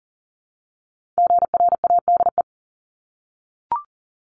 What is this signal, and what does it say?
Morse code, heard as a keyed tone, about 42 wpm, 700 Hz. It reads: GRADE
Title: Multi-letter Phonemes 40wpm